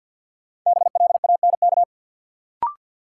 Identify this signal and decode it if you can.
Morse code, heard as a keyed tone, about 53 wpm, 700 Hz. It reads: BLANK
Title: Sentences from Top 1500 Words - New Word Drill - 50wpm